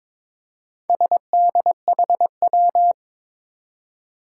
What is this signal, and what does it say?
Morse code, heard as a keyed tone, about 21 wpm, 700 Hz. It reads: SDHW
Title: Four Challenging Letter-Number Combinations 22wpm